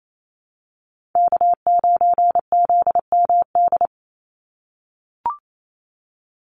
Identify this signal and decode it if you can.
Morse code, heard as a keyed tone, about 28 wpm, 700 Hz. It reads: K9ZMD
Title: NA SOTA Chaser Callsigns 28wpm